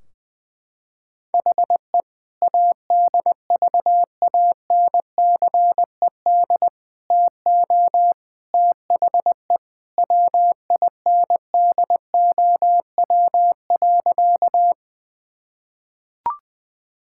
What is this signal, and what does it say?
Morse code, heard as a keyed tone, about 20 wpm, 700 Hz. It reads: HE ADVANCED TO THE WINDOW.
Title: Sentences from Top 1600 Words 20wpm